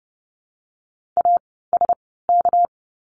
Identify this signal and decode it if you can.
Morse code, heard as a keyed tone, about 30 wpm, spaced as for 17 wpm, 700 Hz. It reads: ASK
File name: course / 30:10wpm / 230 Lesson 37f - Letter x - Words 30wpm